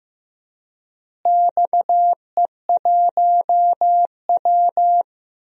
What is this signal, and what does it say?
Morse code, heard as a keyed tone, about 15 wpm, 700 Hz. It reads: XE1W
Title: DXCC-100 Call Signs - 4 Characters - Repeated 2x - 15wpm